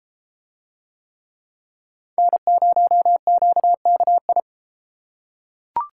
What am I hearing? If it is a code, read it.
Morse code, heard as a keyed tone, about 31 wpm, 700 Hz. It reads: N0QKI